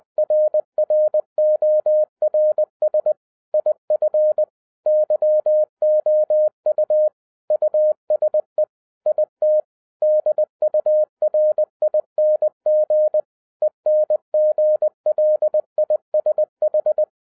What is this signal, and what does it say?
Morse code, heard as a keyed tone, about 20 wpm, 610 Hz. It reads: RRORS IF YOU USE IT DURING ENGLISH